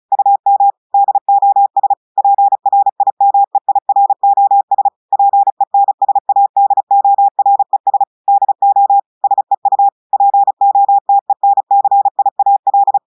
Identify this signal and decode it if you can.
Morse code, heard as a keyed tone, about 35 wpm, 800 Hz. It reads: UM DOS PRIMEIROS PENSADORES DO SEU POTENCIAL
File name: audios / portugais / portugais texte en clair 1 vitesse 35 mots minutes